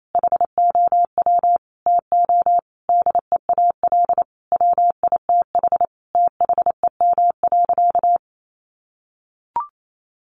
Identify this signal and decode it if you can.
Morse code, heard as a keyed tone, about 28 wpm, 700 Hz. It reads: HOW TO DEAL WITH THEM.
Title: Sentences from Top 800 Words 28wpm